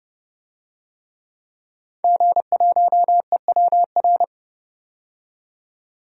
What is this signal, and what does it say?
Morse code, heard as a keyed tone, about 30 wpm, 700 Hz. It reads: G1EWR
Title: DXCC-100 Call Signs - 5 Characters - 30wpm